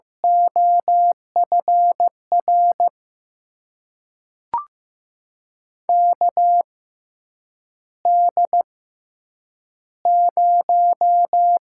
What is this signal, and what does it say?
Morse code, heard as a keyed tone, about 15 wpm, spaced as for 8 wpm, 700 Hz. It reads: OFR KD0
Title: USA Call Signs - 6 Characters - 6x Character Spacing - 15wpm